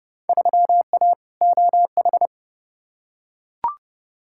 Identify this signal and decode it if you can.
Morse code, heard as a keyed tone, about 30 wpm, 700 Hz. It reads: 3A OH
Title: Field Day Exchanges - 30wpm